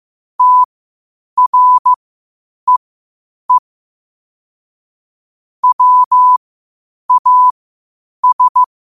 Morse code, heard as a keyed tone, about 15 wpm, spaced as for 8 wpm, 1000 Hz. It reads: TREE WAS
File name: qotdfiles / 8wpm